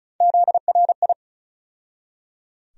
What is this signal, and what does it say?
Morse code, heard as a keyed tone, about 36 wpm, 700 Hz. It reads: ZRI
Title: USA Call Sign Postfixes - 3 Letters 35wpm